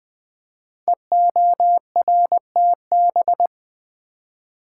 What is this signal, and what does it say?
Morse code, heard as a keyed tone, about 20 wpm, 700 Hz. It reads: EORTB
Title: Code Groups - 5 Letter:Numbers - 20wpm